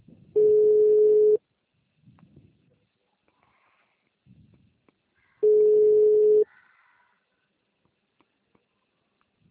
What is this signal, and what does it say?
Ringback tone (Europe, 425 Hz).
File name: out- -2009- - - .23688